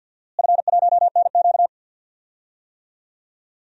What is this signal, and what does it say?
Morse code, heard as a keyed tone, about 53 wpm, 700 Hz. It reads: F1NQ
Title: DXCC-100 Call Signs - 4 Characters - Repeated 2x - 50wpm